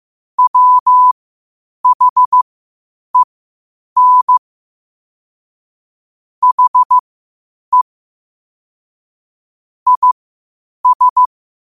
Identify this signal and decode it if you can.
Morse code, heard as a keyed tone, about 15 wpm, spaced as for 8 wpm, 1000 Hz. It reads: WHEN HE IS
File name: qotdfiles / 8wpm